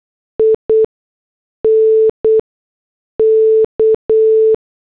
Morse code, heard as a keyed tone, about 8 wpm, spaced as for 6 wpm, 440 Hz. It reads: INK
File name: morse / headlines